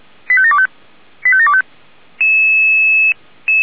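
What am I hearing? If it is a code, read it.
Five-tone selective call (ZVEI), heard as quick runs of five tones, tones of about 70 ms. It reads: 85725 85725
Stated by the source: ZVEI five-tone selective call received on an SDR.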